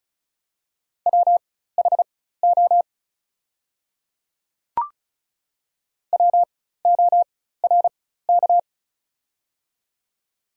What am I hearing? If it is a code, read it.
Morse code, heard as a keyed tone, about 36 wpm, spaced as for 16 wpm, 690 Hz. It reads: WHO WORK
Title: 4x Character Spacing - Top 500 Words - 35wpm